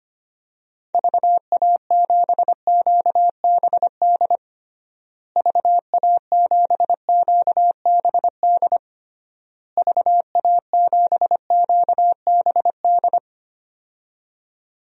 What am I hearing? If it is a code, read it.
Morse code, heard as a keyed tone, about 25 wpm, 700 Hz. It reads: VA7QBD VA7QBD VA7QBD
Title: DXCC-100 Call Signs - 6 Characters - Repeated 3x - 25wpm